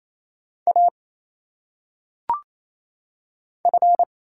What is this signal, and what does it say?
Morse code, heard as a keyed tone, about 27 wpm, 700 Hz. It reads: A F
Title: Single Letters - Phonetic Pronunciation 28wpm